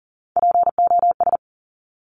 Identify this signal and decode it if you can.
Morse code, heard as a keyed tone, about 40 wpm, 700 Hz. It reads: POS
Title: Top Three Letter Combinations in Words 40wpm